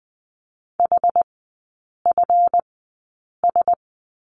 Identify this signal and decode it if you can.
Morse code, heard as a keyed tone, about 20 wpm, spaced as for 8 wpm, 700 Hz. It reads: HFS
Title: 152 Lesson 25e - Letter p - Three Characters 20wpm